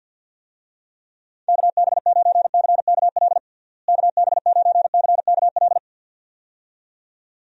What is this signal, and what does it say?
Morse code, heard as a keyed tone, about 50 wpm, 700 Hz. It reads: KB9CKL KB9CKL
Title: USA Call Signs - 6 Characters - Repeated 2x - 50wpm